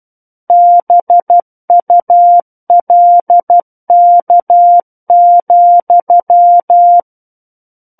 Morse code, heard as a keyed tone, about 12 wpm, 700 Hz. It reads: BULK,
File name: Book One / 03-Chapter 3 - 12 WPM